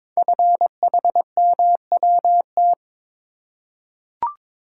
Morse code, heard as a keyed tone, about 22 wpm, 700 Hz. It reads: FHMWT
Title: Code Groups - 5 Letters - 22wpm